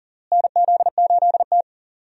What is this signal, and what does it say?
Morse code, heard as a keyed tone, about 38 wpm, 700 Hz. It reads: NZ8T
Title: USA Call Signs - 4 Characters - Repeated 3x - 40wpm